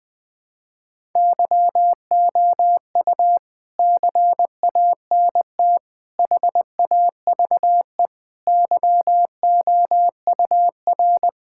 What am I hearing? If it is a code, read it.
Morse code, heard as a keyed tone, about 21 wpm, 700 Hz. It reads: YOU CANT HAVE YOUR
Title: Sets of 9 Words - 20wpm